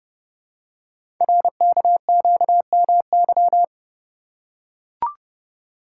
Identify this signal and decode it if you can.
Morse code, heard as a keyed tone, about 30 wpm, 700 Hz. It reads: RKQMY
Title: Code Groups - 5 Letter:Numbers - 30wpm